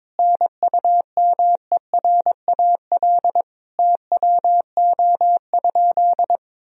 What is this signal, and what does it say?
Morse code, heard as a keyed tone, about 22 wpm, 700 Hz. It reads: NUMERAL TWO?
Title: Sentences from Top 1000 Words - Review Part 2 - 22wpm